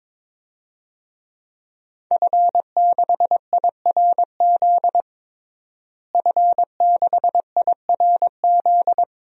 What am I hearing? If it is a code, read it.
Morse code, heard as a keyed tone, about 22 wpm, 700 Hz. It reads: F6IRZ F6IRZ
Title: DXCC-100 Call Signs - 5 Characters - Repeated 2x - 22wpm